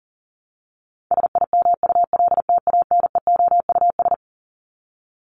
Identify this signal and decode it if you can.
Morse code, heard as a keyed tone, about 40 wpm, 700 Hz. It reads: SIMULTANEOUS
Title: LY Suffix - Top 100 Words with Base Words 40wpm